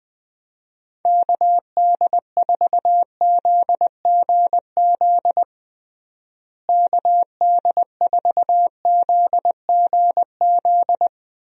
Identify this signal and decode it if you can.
Morse code, heard as a keyed tone, about 21 wpm, 700 Hz. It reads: KD4ZGZ KD4ZGZ
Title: USA Call Signs - 6 Characters - Repeated 3x - 20wpm